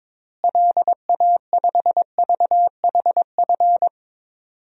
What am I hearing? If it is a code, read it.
Morse code, heard as a keyed tone, about 22 wpm, 700 Hz. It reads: LA5VHF
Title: DXCC-100 Call Signs - 6 Characters - Repeated 3x - 22wpm